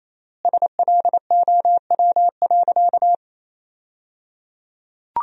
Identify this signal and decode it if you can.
Morse code, heard as a keyed tone, about 28 wpm, 700 Hz. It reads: SLOW.
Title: 3x Word-Spacing - Sentences from Top 400 Words - 28wpm